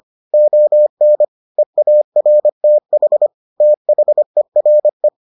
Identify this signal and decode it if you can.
Morse code, heard as a keyed tone, about 25 wpm, 600 Hz. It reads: ON EARTH THERE